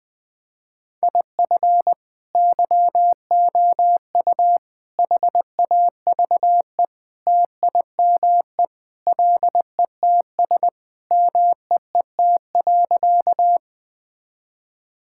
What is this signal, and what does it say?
Morse code, heard as a keyed tone, about 20 wpm, 700 Hz. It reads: IF YOU HAVE TIME LETS MEET.
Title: Sentences with 3-Word Phrases - 20wpm